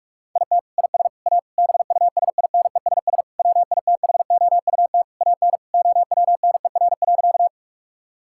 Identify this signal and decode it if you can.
Morse code, heard as a keyed tone, about 46 wpm, 700 Hz. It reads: IT IS A BUSINESS WITHOUT AN OWNER.
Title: Sentences from Top 1600 Words 45wpm